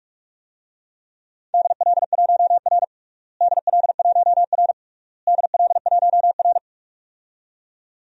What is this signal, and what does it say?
Morse code, heard as a keyed tone, about 48 wpm, 700 Hz. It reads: DL1R DL1R DL1R
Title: DXCC-100 Call Signs - 4 Characters - Repeated 3x - 45wpm